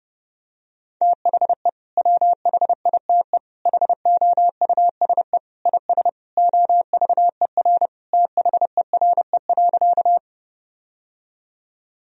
Morse code, heard as a keyed tone, about 30 wpm, 700 Hz. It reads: THE WHITE HOUSE IS OVER THERE.
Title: Sentences from Top 1000 Words - Review Part 1 - 30wpm